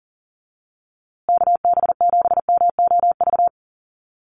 Morse code, heard as a keyed tone, about 40 wpm, 700 Hz. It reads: KB7MOV
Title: USA Call Signs Repeated 3x 40wpm